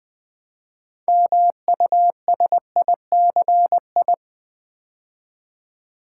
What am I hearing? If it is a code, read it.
Morse code, heard as a keyed tone, about 20 wpm, 690 Hz. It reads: MUSICI
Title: Build A Word - 3 Letters At A Time - 8-Letter Words - 20wpm